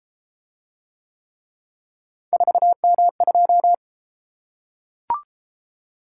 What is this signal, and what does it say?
Morse code, heard as a keyed tone, about 33 wpm, 700 Hz. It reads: 4M2